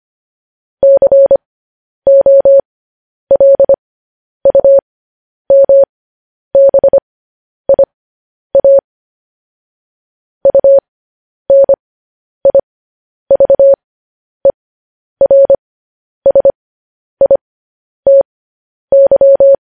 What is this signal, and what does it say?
Morse code, heard as a keyed tone, about 25 wpm, spaced as for 10 wpm, 550 Hz. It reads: COLUMBIA UNIVERSITY